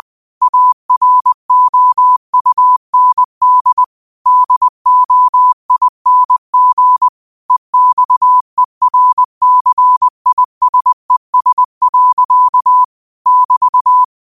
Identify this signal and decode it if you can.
Morse code, heard as a keyed tone, about 20 wpm, 1000 Hz. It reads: AROUND DOING EXERCISES. <BT>
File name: qotdfiles / 20wpm